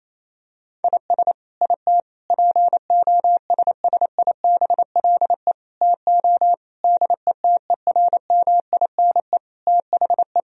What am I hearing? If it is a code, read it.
Morse code, heard as a keyed tone, about 28 wpm, 700 Hz. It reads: IS IT POSSIBLE TO DETERMINE THE
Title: Sentences from Top 1000 Words - Review Part 4 - 28wpm